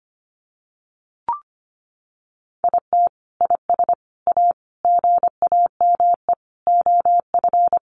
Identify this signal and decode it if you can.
Morse code, heard as a keyed tone, about 25 wpm, 700 Hz. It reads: IT IS A GAME OF
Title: Sentences from Top 1200 Words 25wpm